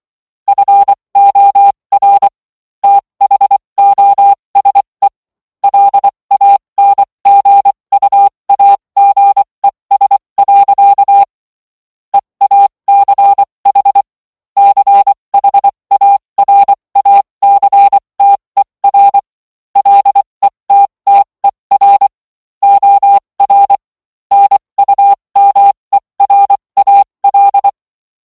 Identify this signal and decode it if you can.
Morse code, heard as a keyed tone, about 24 wpm, spaced as for 20 wpm, 800 Hz. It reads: FOR THOSE LANGUAGES. EACH CHARACTER LETTER OR NUMERAL